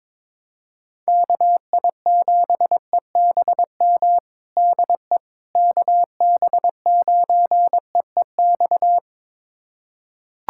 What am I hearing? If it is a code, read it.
Morse code, heard as a keyed tone, about 22 wpm, 700 Hz. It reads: KI7EBM DE KB9EEX
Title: USA Callsign DE Callsign - 6 Characters - 22wpm